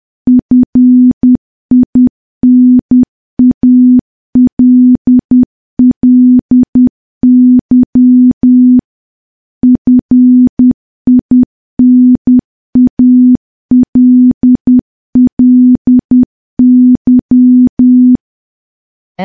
Morse code, heard as a keyed tone, about 10 wpm, 250 Hz. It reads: FINALLY FINALLY
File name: morse / morse-training-10WPM